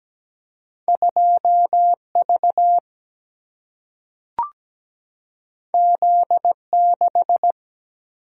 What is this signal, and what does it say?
Morse code, heard as a keyed tone, about 17 wpm, 700 Hz. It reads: 2V Z6
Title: Single-Letter Single-Number 17wpm